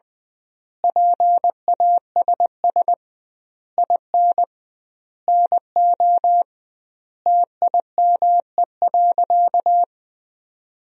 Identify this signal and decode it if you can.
Morse code, heard as a keyed tone, about 20 wpm, 700 Hz. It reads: PASS IN NO TIME.
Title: 2x Word-Spacing - Sentences from Top 400 Words - 20wpm